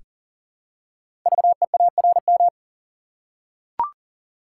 Morse code, heard as a keyed tone, about 42 wpm, 700 Hz. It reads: VEARM
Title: Code Groups - 5 Letters - 40wpm